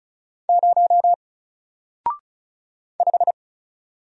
Morse code, heard as a keyed tone, about 35 wpm, 700 Hz. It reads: 0 5
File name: Single-Letter-Number-35wpm